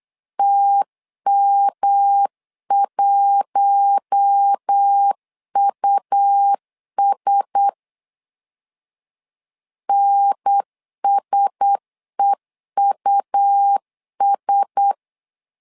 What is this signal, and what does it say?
Morse code, heard as a keyed tone, about 8 wpm, 800 Hz. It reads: TM1US NSEUS
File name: LEZIONI / 4 Q1S 8wpm 2710